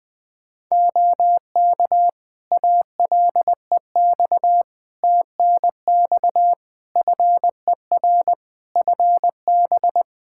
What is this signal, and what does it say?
Morse code, heard as a keyed tone, about 20 wpm, 700 Hz. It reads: OK ALEX TNX FER FB
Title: Practice QSO - Exchange 4 - Ending - 20wpm